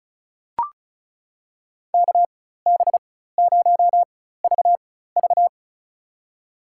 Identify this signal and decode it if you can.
Morse code, heard as a keyed tone, about 36 wpm, spaced as for 16 wpm, 700 Hz. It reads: KB0VV
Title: USA Call Signs - 4x Character Spacing - 35wpm